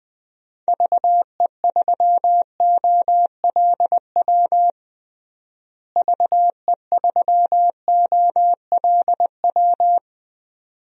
Morse code, heard as a keyed tone, about 21 wpm, 700 Hz. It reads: VE3OLW VE3OLW
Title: DXCC-100 Call Signs - 6 Characters - Repeated 2x - 20wpm